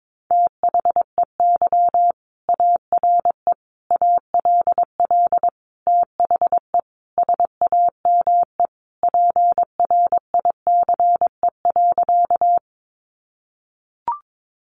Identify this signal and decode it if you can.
Morse code, heard as a keyed tone, about 22 wpm, 700 Hz. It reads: THEY ARE ALL THE SAME PRICE.
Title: Sentences from Top 600 Words 22wpm